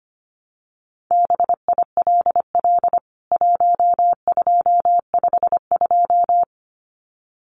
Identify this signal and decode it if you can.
Morse code, heard as a keyed tone, about 25 wpm, 700 Hz. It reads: BILL 1252
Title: CWT Exchange 25wpm